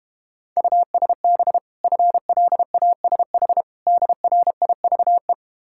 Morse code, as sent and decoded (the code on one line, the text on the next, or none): ..- ... -... / ..-. .-.. .- ... .... / -.. .-. .. ...- .
USB FLASH DRIVE